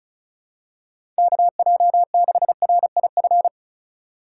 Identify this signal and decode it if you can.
Morse code, heard as a keyed tone, about 36 wpm, 700 Hz. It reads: KJ6RIF
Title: USA Call Signs - 6 Characters - 6x Character Spacing - 35wpm